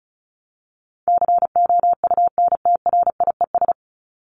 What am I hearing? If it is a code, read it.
Morse code, heard as a keyed tone, about 34 wpm, 700 Hz. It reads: COUNTRIES
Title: ES:S Suffix - Top 100 Words with Base Words 35wpm